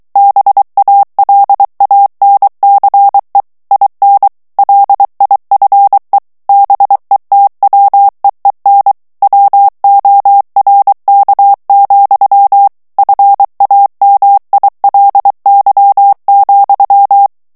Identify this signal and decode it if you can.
Morse code, heard as a keyed tone, about 23 wpm, 800 Hz. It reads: BALANCE IN LIFE BETWEEN WORK, FAMILY,
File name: morse mp3 / ANGLAIS texte clair 1 vitesse 22 mots minute